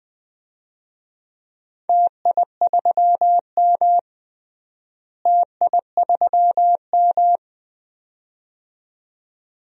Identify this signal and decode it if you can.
Morse code, heard as a keyed tone, about 20 wpm, 700 Hz. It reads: TI3M TI3M
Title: DXCC-100 Call Signs - 4 Characters - Repeated 2x - 20wpm